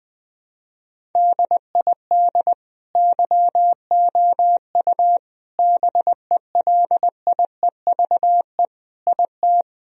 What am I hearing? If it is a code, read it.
Morse code, heard as a keyed tone, about 21 wpm, 700 Hz. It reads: DID YOU BELIEVE IT
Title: Sentences from Top 1000 Words - Review Part 1 - 20wpm